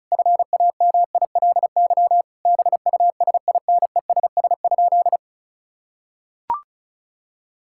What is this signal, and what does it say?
Morse code, heard as a keyed tone, about 34 wpm, 700 Hz. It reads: FAMILY BUSINESS?
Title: Sentences from Top 1000 Words - Review Part 6 - 35wpm